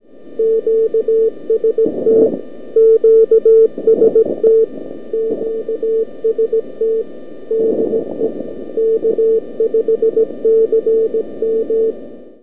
Morse code, heard as a keyed tone, about 17 wpm, 450 Hz. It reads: QST QST QST DE K5CM
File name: k5cm-160m-by-abi-jan-2011